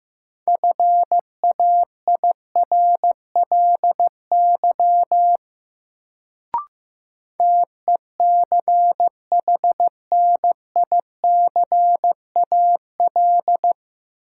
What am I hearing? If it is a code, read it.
Morse code, heard as a keyed tone, about 16 wpm, 700 Hz. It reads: FAIRLY TECHNICAL
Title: Sentences from Top 1100 Words - New Word Drill - 15wpm